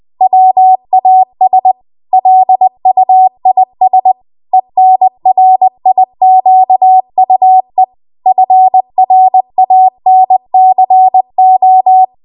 Morse code, heard as a keyed tone, about 20 wpm, 750 Hz. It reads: WAS LUIS ENRIQUE FRANCO
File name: Archive / 20 WPM / 20WPM